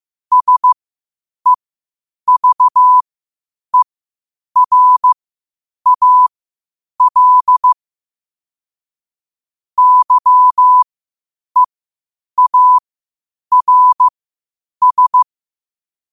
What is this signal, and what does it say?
Morse code, heard as a keyed tone, about 15 wpm, spaced as for 8 wpm, 1000 Hz. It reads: SEVERAL YEARS
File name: qotdfiles / 8wpm